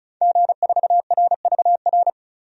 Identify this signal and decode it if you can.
Morse code, heard as a keyed tone, about 34 wpm, 700 Hz. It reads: G4RVR
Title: DXCC-100 Call Signs - 5 Characters - 35wpm